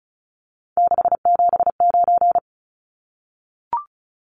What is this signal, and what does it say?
Morse code, heard as a keyed tone, about 34 wpm, 700 Hz. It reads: 679
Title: Numbers - 3-Digit 35wpm